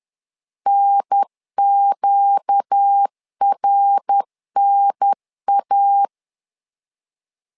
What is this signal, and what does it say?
Morse code, heard as a keyed tone, about 10 wpm, 800 Hz. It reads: NQRNA
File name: LEZIONI / 4 Q 10wpm 2210